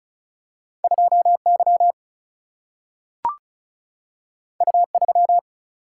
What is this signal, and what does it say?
Morse code, heard as a keyed tone, about 36 wpm, 700 Hz. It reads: 2 Y U 3
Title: Single-Letter Single-Number 35wpm